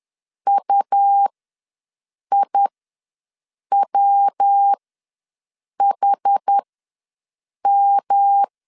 Morse code, heard as a keyed tone, about 10 wpm, spaced as for 6 wpm, 800 Hz. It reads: UIWHM
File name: LEZIONI / 7 GF9Z / 7 GF9Z ws 10wpm 3919